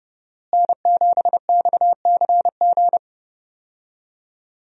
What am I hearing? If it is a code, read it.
Morse code, heard as a keyed tone, about 30 wpm, 700 Hz. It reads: N7XCG
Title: USA Call Signs - 5 Characters - Repeated 3x - 30wpm